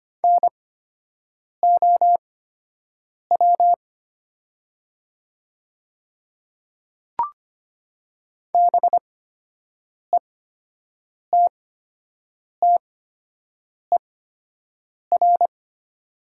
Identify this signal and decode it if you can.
Morse code, heard as a keyed tone, about 25 wpm, spaced as for 7 wpm, 700 Hz. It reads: NOW BETTER
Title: 8x Character Spacing - Top 500 Words - 25wpm